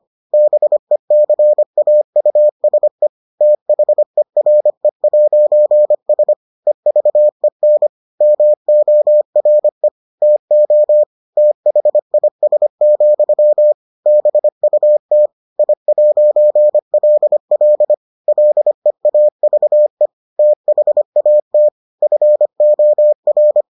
Morse code, heard as a keyed tone, about 25 wpm, 600 Hz. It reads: BECAUSE THERE'S EVEN MORE TO THIS, BUT I'LL LEAVE THAT FOR